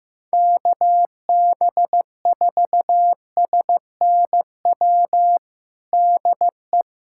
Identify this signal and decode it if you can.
Morse code, heard as a keyed tone, about 15 wpm, 700 Hz. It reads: KB4SNW DE
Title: USA Callsign DE Callsign 15wpm